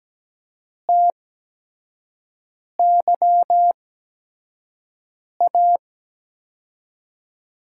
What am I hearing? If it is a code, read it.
Morse code, heard as a keyed tone, about 17 wpm, spaced as for 5 wpm, 700 Hz. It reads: TYA
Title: USA Call Signs - 6 Characters - 8x Character Spacing - 17wpm